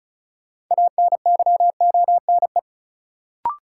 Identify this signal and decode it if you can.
Morse code, heard as a keyed tone, about 36 wpm, 700 Hz. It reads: ANYONE
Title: Sentences from Top 900 Words - New Word Drill - 35wpm